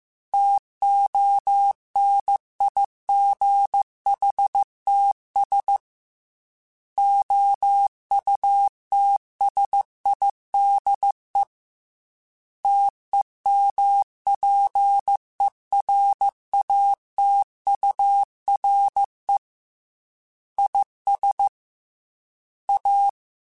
Morse code, heard as a keyed tone, about 15 wpm, 800 Hz. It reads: TONIGHTS OUTSIDE TEMPERATURE IS A
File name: Weather Sets / Weather Sets-mp3s / Weather 4 10 @13